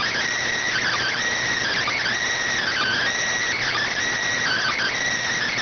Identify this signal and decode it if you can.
Emergency Alert System (SAME) tones (header bursts), heard as harsh buzzing bursts. It is heard from 0.6 to 5.0 s.